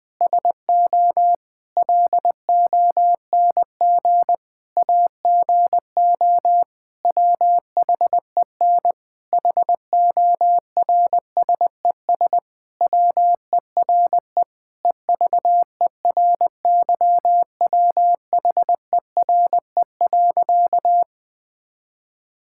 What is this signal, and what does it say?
Morse code, heard as a keyed tone, about 20 wpm, 700 Hz. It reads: SO LONG AGO WHEN HORSES WERE EVERYWHERE.
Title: Sentences with 8-Word Phrases - 20wpm